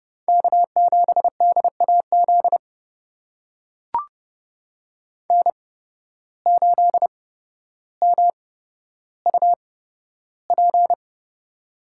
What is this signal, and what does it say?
Morse code, heard as a keyed tone, about 31 wpm, spaced as for 13 wpm, 700 Hz. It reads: K7DAZ N8MUP
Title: USA Call Signs - 5 Characters - 8x Character Spacing - 30wpm